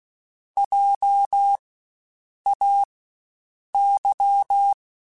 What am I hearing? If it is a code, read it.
Morse code, heard as a keyed tone, about 16 wpm, spaced as for 7 wpm, 800 Hz. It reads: JAY
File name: Spaced Alphas / Spaced Alphas mp3s / Js sp @15